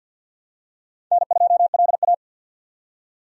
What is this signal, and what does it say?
Morse code, heard as a keyed tone, about 52 wpm, 700 Hz. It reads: N2LA
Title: USA Call Signs - 4 Characters - 50wpm